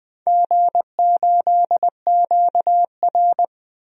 Morse code, heard as a keyed tone, about 20 wpm, 700 Hz. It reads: G8QR
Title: DXCC-100 Call Signs - 4 Characters - 20wpm